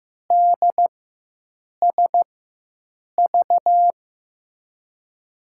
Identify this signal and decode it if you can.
Morse code, heard as a keyed tone, about 16 wpm, spaced as for 7 wpm, 700 Hz. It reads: DSV
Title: USA Call Signs - 6 Characters - 4x Character Spacing - 15wpm